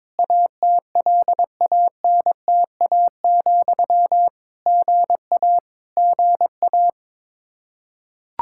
Morse code, heard as a keyed tone, about 22 wpm, 700 Hz. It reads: ATLANTA, GA GA
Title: K1USN SST QSO Practice 22wpm